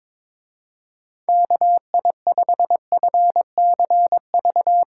Morse code, heard as a keyed tone, about 22 wpm, 700 Hz. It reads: KI5FCV
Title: Speed Racing - USA 2x3 Call Signs 22wpm